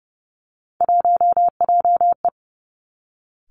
Morse code, heard as a keyed tone, about 30 wpm, 700 Hz. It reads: 1JE
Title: Three Challenging Letter-Number Combinations 30wpm